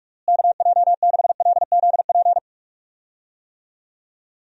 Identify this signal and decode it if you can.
Morse code, heard as a keyed tone, about 46 wpm, 700 Hz. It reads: KJ6LZP
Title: USA Call Signs - 6 Characters - Repeated 2x - 45wpm